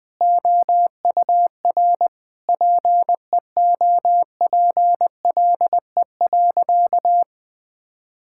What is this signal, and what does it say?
Morse code, heard as a keyed tone, about 20 wpm, 700 Hz. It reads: OUR PEOPLE.
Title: Sentences from Top 1000 Words - Review Part 6 - 20wpm